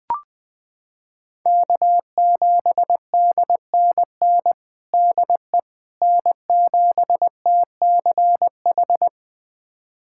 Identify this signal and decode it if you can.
Morse code, heard as a keyed tone, about 21 wpm, 700 Hz. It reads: K7DNN DE N7TCH
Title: USA Callsign DE Callsign - 5 Characters - 20wpm